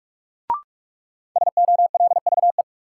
Morse code, heard as a keyed tone, about 47 wpm, 700 Hz. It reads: SOLVE
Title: Sentences from Top 1000 Words - New Word Drill - 45wpm